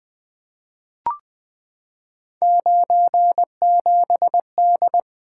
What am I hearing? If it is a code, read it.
Morse code, heard as a keyed tone, about 20 wpm, 700 Hz. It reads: 97D
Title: Three Letter-Number Combinations - 20wpm